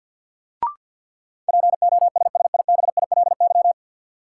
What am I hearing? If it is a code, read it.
Morse code, heard as a keyed tone, about 50 wpm, 700 Hz. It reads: POSSIBILY
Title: Sentences from Top 1300 Words - New Word Drill Alt - 50wpm